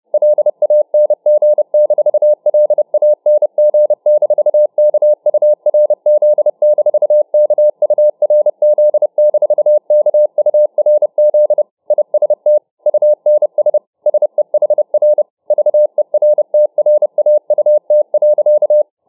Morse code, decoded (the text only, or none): LANG-LANG-KURZ-KURZ-KURZ IST UNS SEHR VERTRAUT.